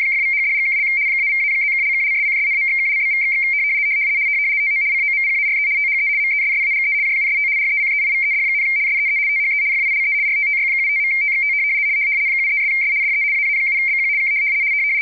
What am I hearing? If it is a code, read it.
RTTY, about 75 baud. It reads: !=#$&() -=-+()()':',./()?=// TO COMPARE THE MESSAGE JUST RECEIVED, SEE THE FILE RTTY75.TXT 73'S DE AA7BQ K
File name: rtty170 / rtty75.wav / rtty75